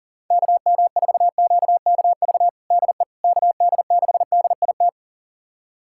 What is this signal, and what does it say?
Morse code, heard as a keyed tone, about 40 wpm, 700 Hz. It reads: KM4QKV DE KD6DIT
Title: USA Callsign DE Callsign - 6 Characters - 40wpm